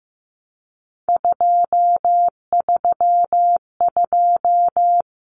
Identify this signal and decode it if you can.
Morse code, heard as a keyed tone, about 15 wpm, 700 Hz. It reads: 232
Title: Numbers - 3-Digit 15wpm